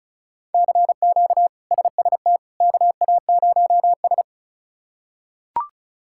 Morse code, heard as a keyed tone, about 34 wpm, 700 Hz. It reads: CQ SST KA0S
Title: K1USN SST QSO Practice - 35wpm